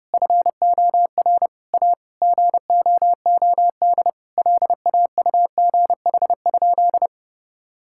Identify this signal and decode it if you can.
Morse code, heard as a keyed tone, about 30 wpm, 700 Hz. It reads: FOR A GOOD LAUGH?
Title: Sentences from Top 1000 Words - Review Part 5 - 30wpm